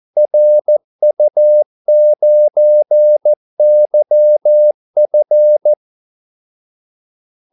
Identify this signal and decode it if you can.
Morse code, heard as a keyed tone, about 14 wpm, 600 Hz. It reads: RU9YF
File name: Callsigns-14wpm